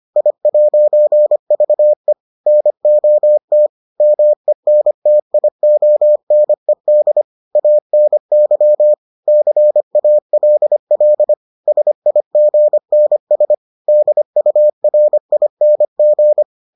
Morse code, heard as a keyed tone, about 25 wpm, 600 Hz. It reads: I'VE NOT MENTIONED ANY CALL SIGNS DURING